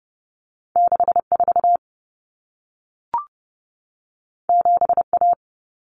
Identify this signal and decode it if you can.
Morse code, heard as a keyed tone, about 30 wpm, 700 Hz. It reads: 64 7A
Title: Common Cut Numbers - 2-Digit 30wpm